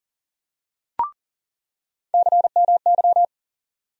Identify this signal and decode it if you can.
Morse code, heard as a keyed tone, about 40 wpm, 700 Hz. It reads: CMY
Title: USA Call Sign Postfixes - 3 Letters 40wpm